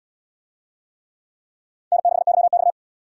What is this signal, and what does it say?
Morse code, heard as a keyed tone, about 68 wpm, 700 Hz. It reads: N6YX